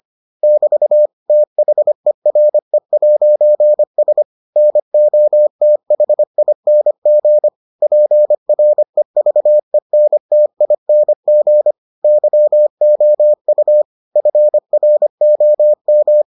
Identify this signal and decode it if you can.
Morse code, heard as a keyed tone, about 25 wpm, 600 Hz. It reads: <BT> THERE'S NOTHING PREVENTING YOU FROM